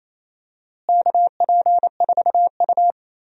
Morse code, heard as a keyed tone, about 28 wpm, 700 Hz. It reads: KP4U
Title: DXCC-100 Call Signs - 4 Characters - Repeated 2x - 28wpm